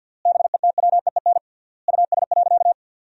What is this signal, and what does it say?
Morse code, heard as a keyed tone, about 50 wpm, 700 Hz. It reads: BETWEEN US.
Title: 3x Word-Spacing - Sentences from Top 500 Words - 50wpm